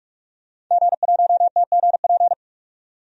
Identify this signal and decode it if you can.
Morse code, heard as a keyed tone, about 46 wpm, 700 Hz. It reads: G1TGP
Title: DXCC-100 Call Signs - 5 Characters - 45wpm